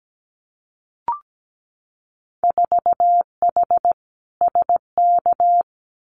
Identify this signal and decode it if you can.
Morse code, heard as a keyed tone, about 17 wpm, 700 Hz. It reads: 4H SK
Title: Winter Field Day Exchanges Encore 17wpm